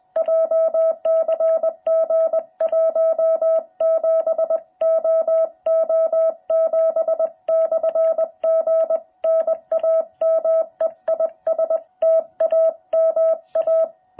Morse code, heard as a keyed tone, about 20 wpm, 660 Hz. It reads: JCG 17OO7/G NAME IS TAMA